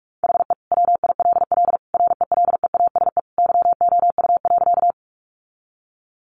Morse code, heard as a keyed tone, about 44 wpm, 700 Hz. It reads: HE WILL RELEASE YOU.
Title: Sentences from Top 1100 Words 45wpm